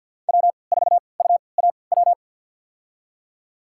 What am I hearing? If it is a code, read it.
Morse code, heard as a keyed tone, about 50 wpm, spaced as for 28 wpm, 700 Hz. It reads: W4UAW
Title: USA Call Signs - 3x Character Spacing - 50wpm